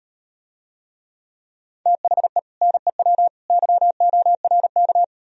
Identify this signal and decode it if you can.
Morse code, heard as a keyed tone, about 38 wpm, 700 Hz. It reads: THE NEW YORK